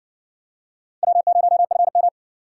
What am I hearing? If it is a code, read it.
Morse code, heard as a keyed tone, about 58 wpm, 700 Hz. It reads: W0FM